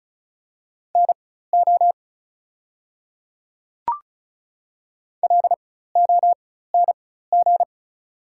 Morse code, heard as a keyed tone, about 34 wpm, spaced as for 16 wpm, 700 Hz. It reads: NO LONG
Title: 4x Character Spacing - Top 100 Words - 35wpm